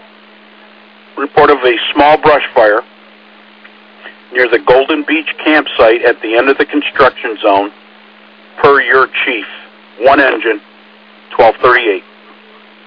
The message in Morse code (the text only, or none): none